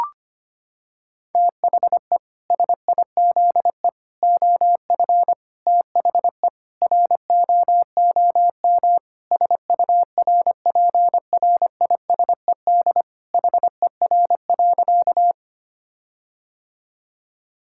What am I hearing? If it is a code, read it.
Morse code, heard as a keyed tone, about 25 wpm, 700 Hz. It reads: THE SIZE OF THE ROOM SURPRISED HER.
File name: practice / 25wpm / Sentences with 4-Word Phrases - 25wpm